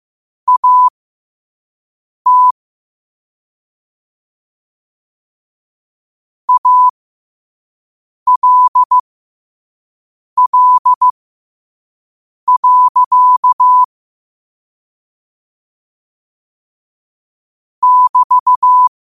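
Morse code, heard as a keyed tone, about 15 wpm, spaced as for 5 wpm, 1000 Hz. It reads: AT ALL. <BT>
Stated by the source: Morse quote-of-the-day practice at 5 wpm.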